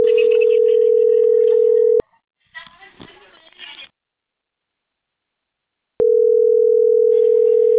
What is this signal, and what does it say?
Ringback tone (North America), 440 + 480 Hz.